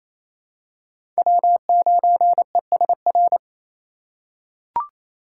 Morse code, heard as a keyed tone, about 28 wpm, 700 Hz. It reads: W9ESR
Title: USA Call Signs - 3x Character Spacing - 28wpm